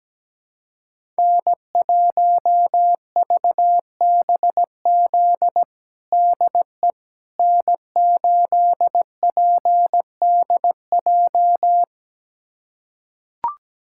Morse code, heard as a keyed tone, about 17 wpm, 700 Hz. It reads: N1VBZ DE N8PDJ
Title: USA Callsign DE Callsign - 5 Characters - 17wpm